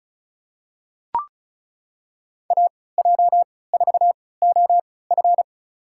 Morse code, heard as a keyed tone, about 36 wpm, spaced as for 20 wpm, 700 Hz. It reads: AJ4OF
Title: USA Call Signs - 5 Characters - 3x Character Spacing - 35wpm